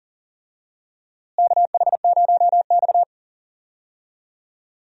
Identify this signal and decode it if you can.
Morse code, heard as a keyed tone, about 40 wpm, 700 Hz. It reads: KH0X
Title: USA Call Signs - 4 Characters - Repeated 2x - 40wpm